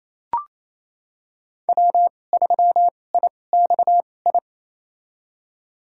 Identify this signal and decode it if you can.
Morse code, heard as a keyed tone, about 28 wpm, spaced as for 20 wpm, 700 Hz. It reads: W3IXI
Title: USA Call Signs - 5 Characters - 2x Character Spacing - 28wpm